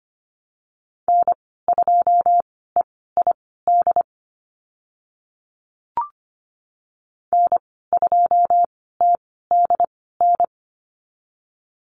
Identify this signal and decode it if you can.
Morse code, heard as a keyed tone, about 25 wpm, spaced as for 16 wpm, 700 Hz. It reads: N2EID N2TDN
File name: Lesson 15g - Number 5 - Call Signs 25wpm